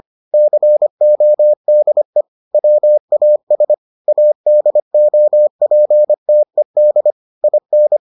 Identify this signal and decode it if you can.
Morse code, heard as a keyed tone, about 25 wpm, 600 Hz. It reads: CODE WAS ADOPTED IN